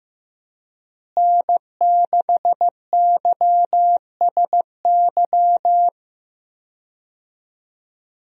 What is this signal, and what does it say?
Morse code, heard as a keyed tone, about 15 wpm, 700 Hz. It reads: N6YSY
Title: USA Call Signs - 5 Characters - Repeated 2x - 15wpm